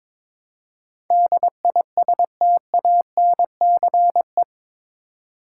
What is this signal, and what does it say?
Morse code, heard as a keyed tone, about 22 wpm, 700 Hz. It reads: DISTANCE
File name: practice / 22wpm / Eight Letter Words - 22wpm